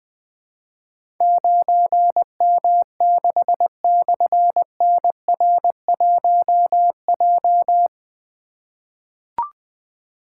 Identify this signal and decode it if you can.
Morse code, heard as a keyed tone, about 20 wpm, 700 Hz. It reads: 9M6/NR1J
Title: Difficult Call Signs 20wpm